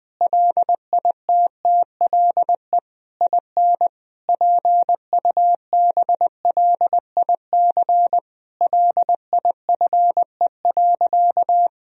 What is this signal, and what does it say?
Morse code, heard as a keyed tone, about 20 wpm, 700 Hz. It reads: LITTLE IN PUBLIC LIFE.
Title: Sentences from Top 1000 Words - Review Part 3 - 20wpm